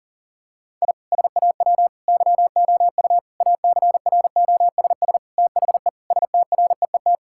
Morse code, heard as a keyed tone, about 37 wpm, 700 Hz. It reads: I SAW YOU ACROSS THE STREET
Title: Sentences from Top 1000 Words - Review Part 5 - 40wpm